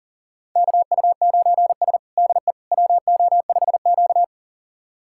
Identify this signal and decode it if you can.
Morse code, heard as a keyed tone, about 40 wpm, 700 Hz. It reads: KU9S DE WO5Q
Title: USA Callsign DE Callsign - 4 Characters - 40wpm